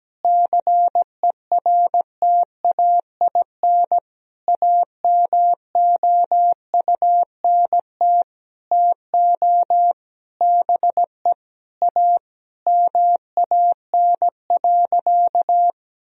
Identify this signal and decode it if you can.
Morse code, heard as a keyed tone, about 17 wpm, 700 Hz. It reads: CERTAIN AMOUNT TO BE A MAN.
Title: Sentences from Top 1000 Words - Review Part 4 - 17wpm